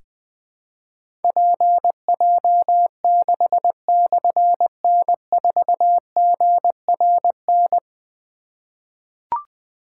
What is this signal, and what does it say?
Morse code, heard as a keyed tone, about 20 wpm, 700 Hz. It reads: PJ6/N4GRN
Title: Difficult Call Signs 20wpm